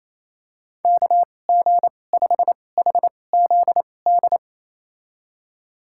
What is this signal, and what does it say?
Morse code, heard as a keyed tone, about 28 wpm, spaced as for 20 wpm, 700 Hz. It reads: KG5HZD
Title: USA Call Signs - 6 Characters - 2x Character Spacing - 28wpm